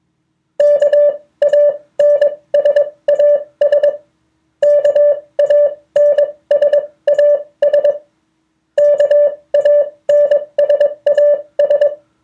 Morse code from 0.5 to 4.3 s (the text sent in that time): KANSAS